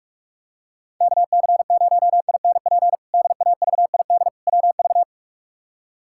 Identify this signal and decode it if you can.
Morse code, heard as a keyed tone, about 46 wpm, 700 Hz. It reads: KC0INP DAVID WV
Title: K1USN SST QSO Practice - 45wpm